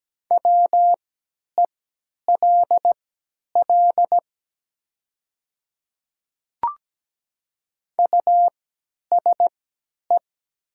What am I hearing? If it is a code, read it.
Morse code, heard as a keyed tone, about 17 wpm, spaced as for 10 wpm, 700 Hz. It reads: WELL USE
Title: 3x Character Spacing - Top 200 Words - 17wpm